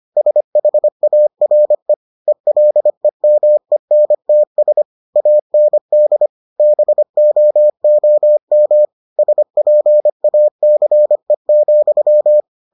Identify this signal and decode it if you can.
Morse code, heard as a keyed tone, about 25 wpm, 600 Hz. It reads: SHARE ELEMENTS AND BOOM SPACE,